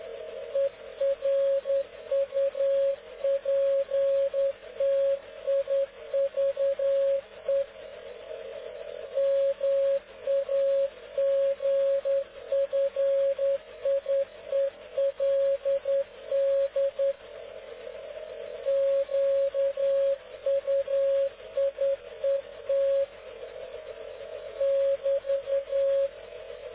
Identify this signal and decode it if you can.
Morse code, heard as a keyed tone, about 10 wpm, 550 Hz. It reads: ERUPTIVE MAGFIELD QUIET <BT>
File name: cw / ubacw dk0wcy slow